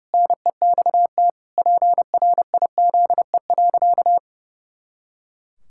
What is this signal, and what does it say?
Morse code, heard as a keyed tone, about 30 wpm, 700 Hz. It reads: NEXT PRIZE.
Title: Pangram Challenge v2 - 30wpm